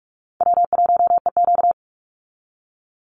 Morse code, heard as a keyed tone, about 44 wpm, 700 Hz. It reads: W1EQ
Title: Call Signs - 45wpm